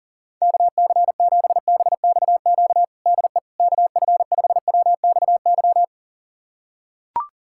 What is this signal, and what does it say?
Morse code, heard as a keyed tone, about 40 wpm, 700 Hz. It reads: KC7BXQ DE KF5WXY
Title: USA Callsign DE Callsign 40wpm